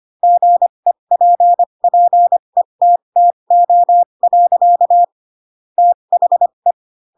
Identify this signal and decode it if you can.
Morse code, heard as a keyed tone, about 25 wpm, spaced as for 22 wpm, 700 Hz. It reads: GEPPETTO. THE
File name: Pinocchio / Pinocchio ch3 18 25 ik1hge